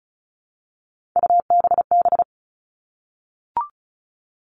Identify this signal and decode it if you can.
Morse code, heard as a keyed tone, about 34 wpm, 700 Hz. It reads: UBB
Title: All Cut Numbers - 3-Digit 35wpm